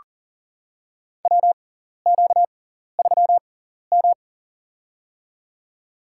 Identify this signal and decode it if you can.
Morse code, heard as a keyed tone, about 40 wpm, spaced as for 14 wpm, 700 Hz. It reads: WQ3M
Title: USA Call Signs - 4 Characters - 6x Character Spacing - 40wpm